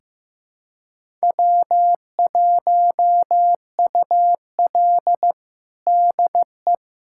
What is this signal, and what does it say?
Morse code, heard as a keyed tone, about 15 wpm, 700 Hz. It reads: W1UL DE
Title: USA Callsign DE Callsign - 4 Characters - 15wpm